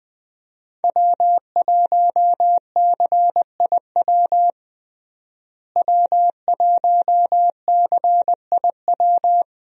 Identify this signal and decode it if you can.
Morse code, heard as a keyed tone, about 21 wpm, 700 Hz. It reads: W1CIW W1CIW
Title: USA Call Signs - 5 Characters - Repeated 2x - 20wpm